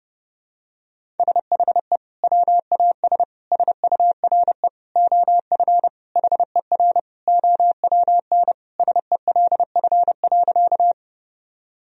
Morse code, heard as a keyed tone, about 32 wpm, 700 Hz. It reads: SHE WAS SURE OF HER OWN SELF.
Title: Sentences from Top 1000 Words - Review Part 1 - 30wpm